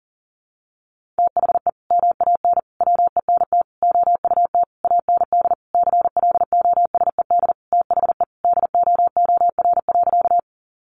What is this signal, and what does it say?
Morse code, heard as a keyed tone, about 40 wpm, 700 Hz. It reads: THE MAN WENT OUT AND CLOSED THE DOOR.
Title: Sentences from Top 1400 Words 40wpm